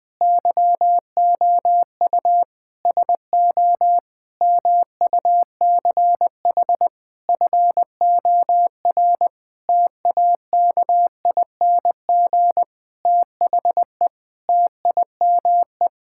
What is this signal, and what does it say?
Morse code, heard as a keyed tone, about 20 wpm, 700 Hz. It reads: YOU SO MUCH FOR TAKING THE TIME
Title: Sentences with 9-Word Phrases - 20wpm